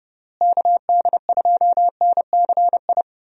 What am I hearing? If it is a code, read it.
Morse code, heard as a keyed tone, about 31 wpm, 700 Hz. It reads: KD2NCI
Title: USA Call Signs - 6 Characters - Repeated 2x - 30wpm